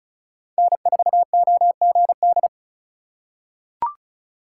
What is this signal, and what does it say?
Morse code, heard as a keyed tone, about 34 wpm, 700 Hz. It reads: N4OGD
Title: USA Call Signs - 5 Characters - 2x Character Spacing - 35wpm